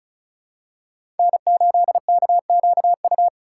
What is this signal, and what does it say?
Morse code, heard as a keyed tone, about 36 wpm, 700 Hz. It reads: N8KQU
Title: USA Call Signs - 5 Characters - 3x Character Spacing - 35wpm